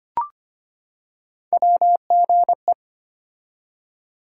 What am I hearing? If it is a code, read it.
Morse code, heard as a keyed tone, about 25 wpm, 700 Hz. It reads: WGE
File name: Three Letter-Number Combinations - 25wpm